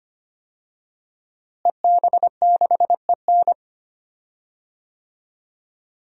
Morse code, heard as a keyed tone, about 25 wpm, 700 Hz. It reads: EB6EN
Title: DXCC-100 Call Signs - 5 Characters - Repeated 2x - 25wpm